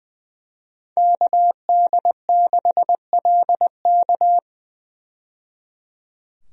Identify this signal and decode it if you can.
Morse code, heard as a keyed tone, about 21 wpm, 700 Hz. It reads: KD6LK
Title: USA Call Signs - 5 Characters - 20wpm